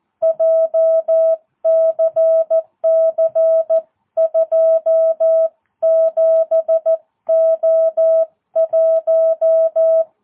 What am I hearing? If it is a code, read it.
Morse code, heard as a keyed tone, about 14 wpm, 650 Hz. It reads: JCC27O1
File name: jh8cbh / 2022cw